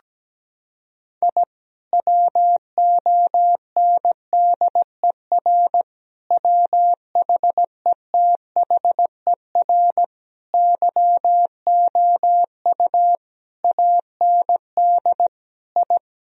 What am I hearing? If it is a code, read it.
Morse code, heard as a keyed tone, about 17 wpm, 700 Hz. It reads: I WONDER WHETHER YOU AND I
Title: Sentences from Top 1000 Words - Review Part 3 - 17wpm